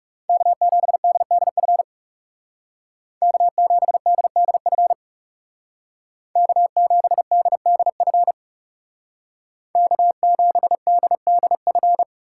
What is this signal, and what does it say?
Morse code, heard as a keyed tone, about 36 wpm, 700 Hz. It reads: K7DDF K7DDF K7DDF K7DDF
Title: Speed Racing - USA 1x3 Call Signs 30wpm